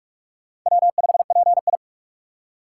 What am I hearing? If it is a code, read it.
Morse code, heard as a keyed tone, about 44 wpm, 700 Hz. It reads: W5PI
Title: USA Call Signs - 4 Characters - 2x Character Spacing - 45wpm